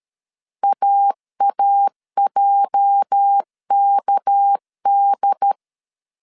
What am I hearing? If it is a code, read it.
Morse code, heard as a keyed tone, about 13 wpm, 800 Hz. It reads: AAJKD
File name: CW / DET / 1 9 1500C S B 12wpm 3819